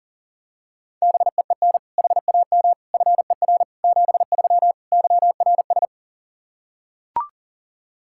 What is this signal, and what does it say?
Morse code, heard as a keyed tone, about 40 wpm, 700 Hz. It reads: BEEN HAM FER 73 YRS
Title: Practice QSO - Exchange 3 - More Info - 40wpm